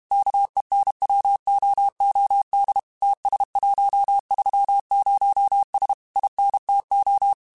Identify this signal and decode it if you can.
Morse code, heard as a keyed tone, about 32 wpm, 800 Hz. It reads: KENWOOD TS130S INTO
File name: QSOs / QSOs-mp3s / WD0FBT 1@30